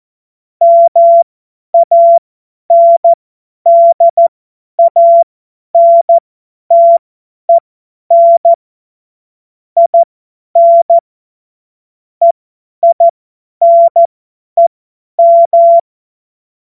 Morse code, heard as a keyed tone, about 14 wpm, spaced as for 10 wpm, 680 Hz. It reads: MANDANTEN IN EINEM
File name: Morsekurs / Witze / Witz 10. Mai 2022 50 F70